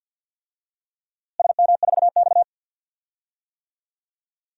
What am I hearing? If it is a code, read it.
Morse code, heard as a keyed tone, about 52 wpm, 700 Hz. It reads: SM4X
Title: DXCC-100 Call Signs - 4 Characters - Repeated 3x - 50wpm